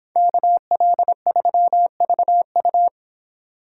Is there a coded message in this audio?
Yes: Morse code from 0.2 to 2.9 s, heard as a keyed tone, about 25 wpm, 700 Hz. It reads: KL3VU